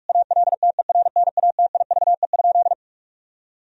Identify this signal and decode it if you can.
Morse code, heard as a keyed tone, about 47 wpm, 700 Hz. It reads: ALTERNATIVE?
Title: Sentences from Top 1000 Words - Review Part 4 - 45wpm